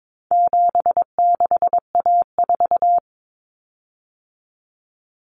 Morse code, heard as a keyed tone, about 22 wpm, 700 Hz. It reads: 76A4
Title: Common Cut Numbers - 4-Digit 22wpm